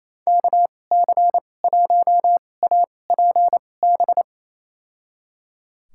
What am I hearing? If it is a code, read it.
Morse code, heard as a keyed tone, about 28 wpm, spaced as for 20 wpm, 700 Hz. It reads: KC1APB
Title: USA Call Signs - 2x Character Spacing - 28wpm